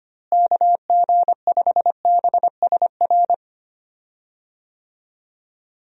Morse code, heard as a keyed tone, about 25 wpm, 700 Hz. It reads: KG5BSR
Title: USA Call Signs - 6 Characters - Repeated 3x - 25wpm